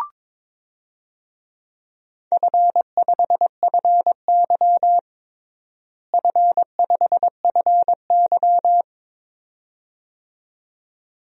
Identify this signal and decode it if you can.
Morse code, heard as a keyed tone, about 22 wpm, 700 Hz. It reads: F5FY F5FY
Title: DXCC-100 Call Signs - 4 Characters - Repeated 2x - 22wpm